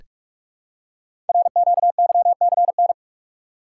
Morse code, heard as a keyed tone, about 44 wpm, 700 Hz. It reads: RQYCN
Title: Code Groups - 5 Letters - 45wpm